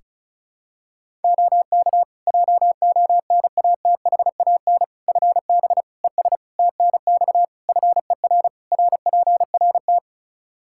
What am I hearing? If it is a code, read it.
Morse code, heard as a keyed tone, about 36 wpm, 700 Hz. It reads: OK JONATHAN FB ES TNX FER RPRT
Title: Practice QSO - Exchange 2 - Info - 35wpm